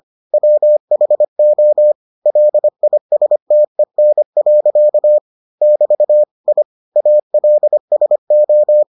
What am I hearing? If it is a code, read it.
Morse code, heard as a keyed tone, about 25 wpm, 600 Hz. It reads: WHO LISTEN. <BT> I ALSO